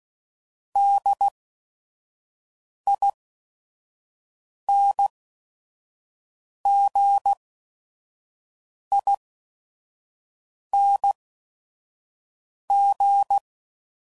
Morse code, heard as a keyed tone, about 16 wpm, spaced as for 5 wpm, 800 Hz. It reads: DINGING